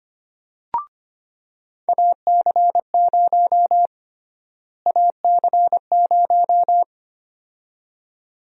Morse code, heard as a keyed tone, about 25 wpm, 700 Hz. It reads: AC0 AC0
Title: USA Call Sign Prefixes - Repeated 2x - 25wpm